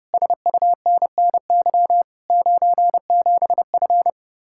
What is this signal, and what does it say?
Morse code, heard as a keyed tone, about 33 wpm, 700 Hz. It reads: SUNNY 97F